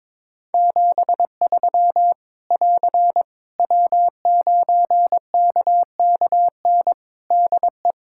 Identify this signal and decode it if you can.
Morse code, heard as a keyed tone, about 22 wpm, 700 Hz. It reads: 73 <AR> W9KKN DE
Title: Practice QSO - Exchange 4 - Ending - 22wpm